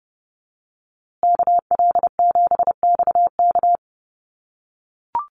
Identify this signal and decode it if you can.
Morse code, heard as a keyed tone, about 30 wpm, 700 Hz. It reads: KL7XK
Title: USA Call Signs Repeated 2x 30wpm